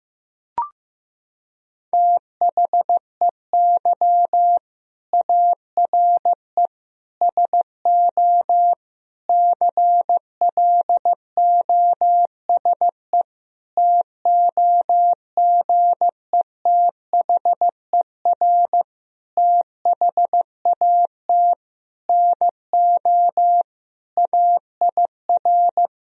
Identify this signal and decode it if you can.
Morse code, heard as a keyed tone, about 15 wpm, 700 Hz. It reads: THEY ARE SO CLOSE TOGETHER THAT NO AIR
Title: Sentences from Top 1000 Words - Review Part 1 - 15wpm